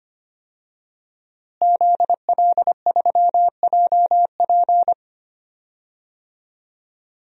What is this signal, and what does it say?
Morse code, heard as a keyed tone, about 25 wpm, 700 Hz. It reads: ZL3JP
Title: DXCC-100 Call Signs - 5 Characters - Repeated 2x - 25wpm